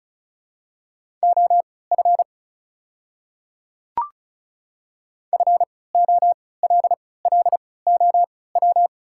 Morse code, heard as a keyed tone, about 36 wpm, spaced as for 20 wpm, 700 Hz. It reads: OF FOLLOW
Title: 3x Character Spacing - Top 200 Words - 35wpm